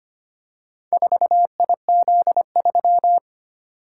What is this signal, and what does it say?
Morse code, heard as a keyed tone, about 25 wpm, 700 Hz. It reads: 4IZ3
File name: practice / 25wpm / Four Letter-Number Combinations - 25wpm